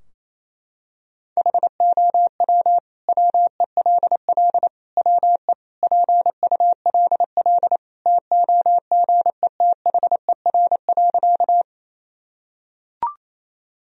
Morse code, heard as a keyed tone, about 27 wpm, 700 Hz. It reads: HOW WELL WE PULL TOGETHER.
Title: Sentences from Top 1000 Words - Review Part 2 - 28wpm